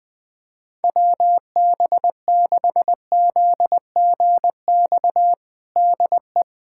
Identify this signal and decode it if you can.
Morse code, heard as a keyed tone, about 20 wpm, 700 Hz. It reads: WB6ZGX DE
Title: USA Callsign DE Callsign 20wpm